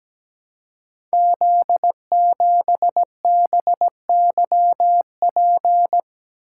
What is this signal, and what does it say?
Morse code, heard as a keyed tone, about 17 wpm, 700 Hz. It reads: Z7BYP
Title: Code Groups - 5 Letter:Numbers - 17wpm